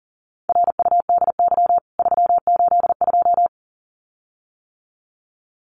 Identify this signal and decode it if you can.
Morse code, heard as a keyed tone, about 40 wpm, 700 Hz. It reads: RUDY 382
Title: CWT Exchange 40wpm